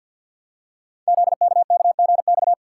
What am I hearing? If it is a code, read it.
Morse code, heard as a keyed tone, about 50 wpm, 700 Hz. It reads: ZKKGX
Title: Code Groups - 5 Letter:Numbers - 50wpm